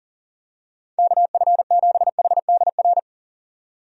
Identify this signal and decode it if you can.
Morse code, heard as a keyed tone, about 40 wpm, 700 Hz. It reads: KF7HDR
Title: USA Call Signs - 6 Characters - 40wpm